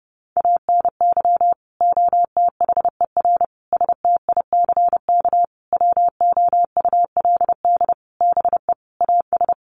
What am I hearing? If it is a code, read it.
Morse code, heard as a keyed tone, about 30 wpm, 700 Hz. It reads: ANY OTHER STICK WOULD BE AS
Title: Sentences from Top 1200 Words 30wpm